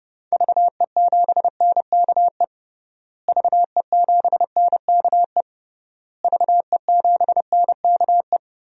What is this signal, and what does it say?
Morse code, heard as a keyed tone, about 32 wpm, 700 Hz. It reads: VE7NKE VE7NKE VE7NKE
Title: DXCC-100 Call Signs - 6 Characters - Repeated 3x - 30wpm